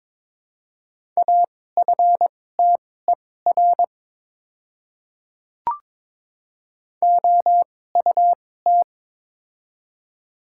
Morse code, heard as a keyed tone, about 22 wpm, spaced as for 16 wpm, 700 Hz. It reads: AFTER OUT
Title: 2x Character Spacing - Top 500 Words - 22wpm